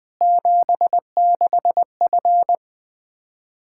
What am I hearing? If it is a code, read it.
Morse code, heard as a keyed tone, about 20 wpm, 700 Hz. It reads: 76F
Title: K1USN SST QSO Practice 20wpm